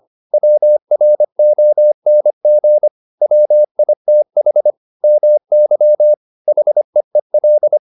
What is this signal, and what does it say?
Morse code, heard as a keyed tone, about 25 wpm, 600 Hz. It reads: WRONG WITH MY HEEL